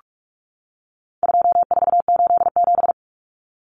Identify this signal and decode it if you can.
Morse code, heard as a keyed tone, about 44 wpm, 700 Hz. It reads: 2487
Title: Numbers - 4-Digit 45wpm